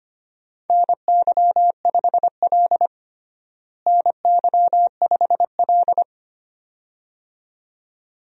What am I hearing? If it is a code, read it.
Morse code, heard as a keyed tone, about 25 wpm, 700 Hz. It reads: NY5L NY5L
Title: USA Call Signs - 4 Characters - Repeated 2x - 25wpm